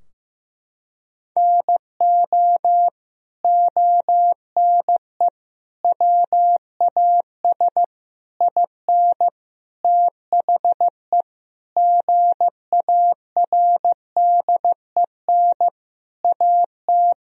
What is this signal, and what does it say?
Morse code, heard as a keyed tone, about 15 wpm, 700 Hz. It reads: NO ONE WAS IN THE GARDEN AT
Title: Sentences from Top 1000 Words - Review Part 4 - 15wpm